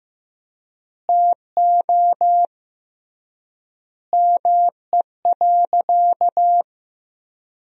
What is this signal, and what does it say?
Morse code, heard as a keyed tone, about 15 wpm, 700 Hz. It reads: TO ME.
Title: 3x Word-Spacing - Sentences from Top 400 Words - 15wpm